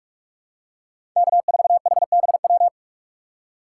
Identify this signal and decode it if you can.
Morse code, heard as a keyed tone, about 44 wpm, 700 Hz. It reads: K4HBW
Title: USA Call Signs - 5 Characters - 6x Character Spacing - 45wpm